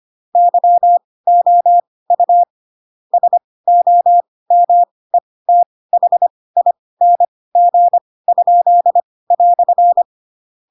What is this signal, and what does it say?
Morse code, heard as a keyed tone, about 25 wpm, spaced as for 18 wpm, 700 Hz. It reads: YOU SOMETHING?"
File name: Pinocchio / Pinocchio ch4 12 25 ik1hge